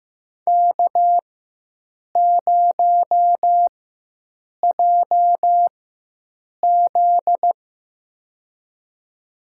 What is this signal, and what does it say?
Morse code, heard as a keyed tone, about 15 wpm, spaced as for 7 wpm, 700 Hz. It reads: K0JZ
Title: USA Call Signs - 4 Characters - 4x Character Spacing - 15wpm